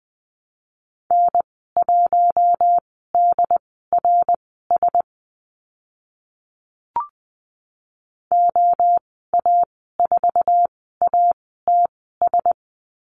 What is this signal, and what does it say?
Morse code, heard as a keyed tone, about 20 wpm, spaced as for 14 wpm, 700 Hz. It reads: N1DRS OA4ATS
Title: 069 Lesson 12g - Letter d - Call Signs 20wpm